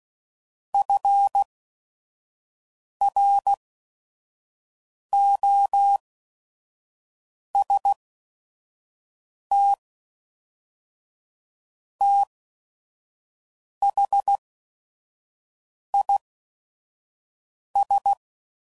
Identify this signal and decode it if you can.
Morse code, heard as a keyed tone, about 16 wpm, spaced as for 5 wpm, 800 Hz. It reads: FROST THIS
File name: CW Code Groups @5WPM / Weather Sets / Weather 4 24